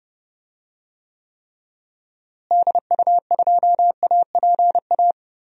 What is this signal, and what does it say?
Morse code, heard as a keyed tone, about 30 wpm, 700 Hz. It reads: DU2APA
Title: DXCC-100 Call Signs - 6 Characters - Repeated 3x - 30wpm